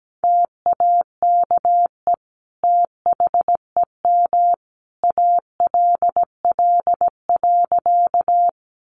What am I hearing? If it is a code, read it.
Morse code, heard as a keyed tone, about 17 wpm, 700 Hz. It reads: TAKE THEM ALL.
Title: Sentences from Top 200 Words 17wpm